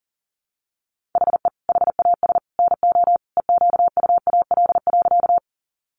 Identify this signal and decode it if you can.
Morse code, heard as a keyed tone, about 40 wpm, 700 Hz. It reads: HE HAS NO EQUAL.
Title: Sentences from Top 1400 Words 40wpm